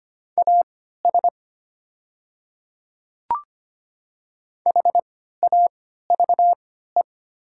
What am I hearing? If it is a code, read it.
Morse code, heard as a keyed tone, about 25 wpm, spaced as for 14 wpm, 700 Hz. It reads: AS HAVE
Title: 3x Character Spacing - Top 100 Words - 25wpm